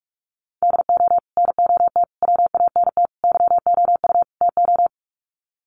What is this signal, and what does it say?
Morse code, heard as a keyed tone, about 44 wpm, 700 Hz. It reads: DO NOT WANT YOU TO
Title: Sets of 5 Words 45wpm